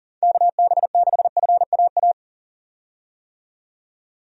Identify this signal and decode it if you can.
Morse code, heard as a keyed tone, about 40 wpm, 700 Hz. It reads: KB6FAA
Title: USA Call Signs - 6 Characters - 40wpm